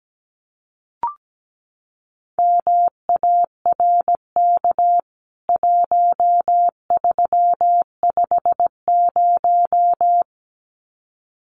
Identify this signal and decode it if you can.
Morse code, heard as a keyed tone, about 17 wpm, 700 Hz. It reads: MARK 1350
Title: CWT Exchange 17wpm